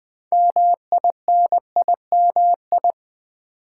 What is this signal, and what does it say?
Morse code, heard as a keyed tone, about 20 wpm, 700 Hz. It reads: MINIMI
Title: Build A Word - 2 Letters At A Time - 8-Letter Words - 20wpm